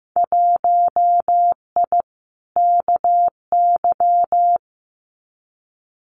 Morse code, heard as a keyed tone, about 15 wpm, 700 Hz. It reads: 1I KY
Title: Winter Field Day Exchanges 15wpm